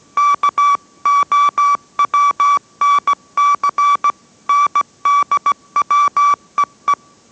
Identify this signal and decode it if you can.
Morse code, heard as a keyed tone, about 17 wpm, 1150 Hz. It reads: KOWNC NDWEE